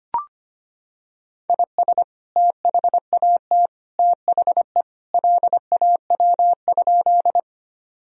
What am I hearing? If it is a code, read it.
Morse code, heard as a keyed tone, about 25 wpm, 700 Hz. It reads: IS THAT THE LAW?
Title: Sentences from Top 1000 Words - Review Part 3 - 25wpm